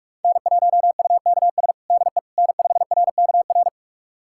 Morse code, heard as a keyed tone, about 46 wpm, 700 Hz. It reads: N1UKS DE N5RKR
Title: USA Callsign DE Callsign - 5 Characters - 45wpm